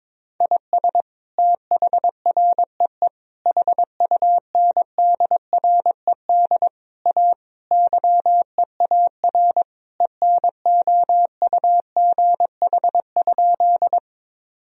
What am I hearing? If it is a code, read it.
Morse code, heard as a keyed tone, about 22 wpm, 700 Hz. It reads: IS THREE HUNDRED A YEAR ENOUGH?
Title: Sentences from Top 1000 Words - Review Part 2 - 22wpm